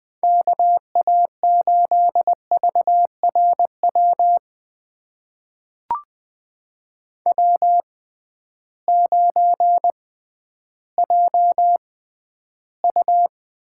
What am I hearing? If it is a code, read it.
Morse code, heard as a keyed tone, about 20 wpm, spaced as for 12 wpm, 700 Hz. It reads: KA8VRW W9JU
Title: USA Call Signs - 6x Character Spacing - 20wpm